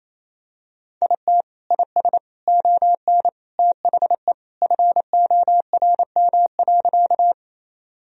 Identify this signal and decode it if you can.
Morse code, heard as a keyed tone, about 28 wpm, 700 Hz. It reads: IT IS ON THE FORM.
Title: Sentences from Top 1000 Words - Review Part 1 - 28wpm